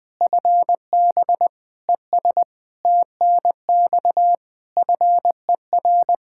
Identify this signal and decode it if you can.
Morse code, heard as a keyed tone, about 21 wpm, 700 Hz. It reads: FB ES TNX FER
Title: K1USN SST QSO Practice 20wpm